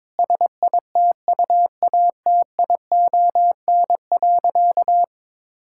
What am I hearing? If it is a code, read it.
Morse code, heard as a keyed tone, about 21 wpm, 700 Hz. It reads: SITUATION.
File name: practice / 22wpm / Sentences with 8-Word Phrases - 22wpm